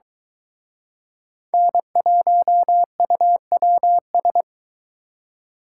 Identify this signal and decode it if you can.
Morse code, heard as a keyed tone, about 23 wpm, 700 Hz. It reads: N1UWS